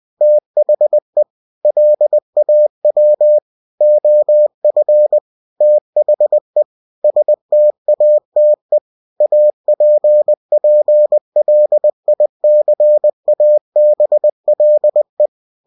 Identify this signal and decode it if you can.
Morse code, heard as a keyed tone, about 20 wpm, 600 Hz. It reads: THE LAW OF THE STATE APPLICABLE
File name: cw-books / winnie-the-pooh / winnie-20wpm-page-0119